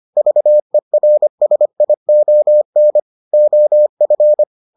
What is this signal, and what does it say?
Morse code, heard as a keyed tone, about 25 wpm, 600 Hz. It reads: VERSION OF